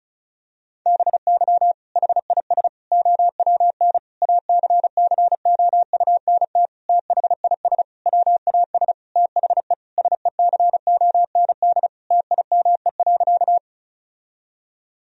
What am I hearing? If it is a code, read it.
Morse code, heard as a keyed tone, about 36 wpm, 700 Hz. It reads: BY HIS OWN ACCOUNT THIS WAS THE SECOND TIME.
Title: Sentences from Top 1000 Words - Review Part 1 - 35wpm